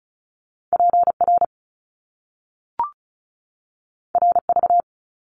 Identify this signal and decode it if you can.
Morse code, heard as a keyed tone, about 36 wpm, 700 Hz. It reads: PR RV
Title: Top Two Letter Combinations in Words 35wpm